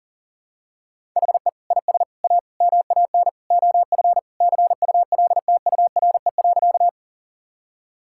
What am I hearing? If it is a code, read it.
Morse code, heard as a keyed tone, about 40 wpm, 700 Hz. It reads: HE IS A MAN OF CULTURE.
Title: Sentences from Top 1000 Words - Review Part 6 - 40wpm